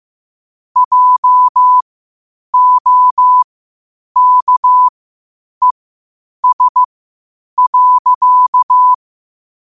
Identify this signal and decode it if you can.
Morse code, heard as a keyed tone, about 15 wpm, spaced as for 8 wpm, 1000 Hz. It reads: JOKES.
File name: qotdfiles / 8wpm